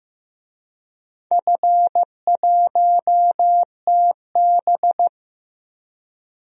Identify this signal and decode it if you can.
Morse code, heard as a keyed tone, about 15 wpm, 700 Hz. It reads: F1TB
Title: DXCC-100 Call Signs - 4 Characters - 15wpm